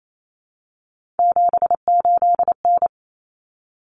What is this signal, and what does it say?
Morse code, heard as a keyed tone, about 28 wpm, 700 Hz. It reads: 78N
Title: Common Cut Numbers - 3-Digit 28wpm